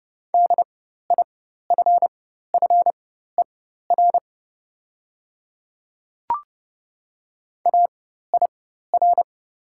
Morse code, heard as a keyed tone, about 31 wpm, spaced as for 14 wpm, 700 Hz. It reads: DIFFER AIR
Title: 4x Character Spacing - Top 300 Words - 30wpm